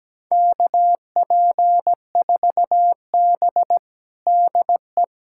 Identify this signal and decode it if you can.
Morse code, heard as a keyed tone, about 17 wpm, 700 Hz. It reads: KP4B DE
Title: USA Callsign DE Callsign - 4 Characters - 17wpm